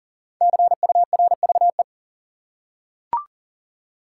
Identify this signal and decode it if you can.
Morse code, heard as a keyed tone, about 38 wpm, 700 Hz. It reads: CURVE
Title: Five Letter Words - 40wpm